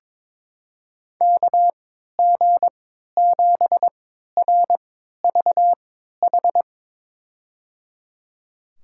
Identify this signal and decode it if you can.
Morse code, heard as a keyed tone, about 22 wpm, spaced as for 13 wpm, 700 Hz. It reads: KG7RVH
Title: USA Call Signs - 6 Characters - 3x Character Spacing - 22wpm